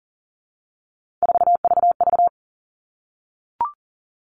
Morse code, heard as a keyed tone, about 40 wpm, 700 Hz. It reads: 4VV
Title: All Cut Numbers - 3-Digit 40wpm